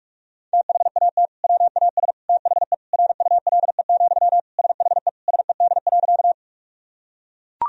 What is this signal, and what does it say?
Morse code, heard as a keyed tone, about 43 wpm, 700 Hz. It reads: THAT WAS THE RULE, SHE SED.
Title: Sentences from Top 400 Words and Abbreviations 45wpm